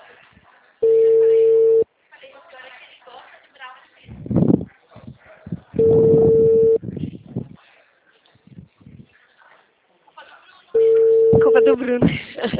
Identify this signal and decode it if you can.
Ringback tone (China, 450 Hz), with speech.
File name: out- -2006